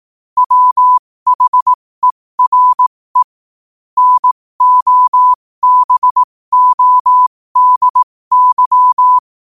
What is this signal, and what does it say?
Morse code, heard as a keyed tone, about 18 wpm, spaced as for 15 wpm, 1000 Hz. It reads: WHERE NOBODY
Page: qotd-at-15-wpm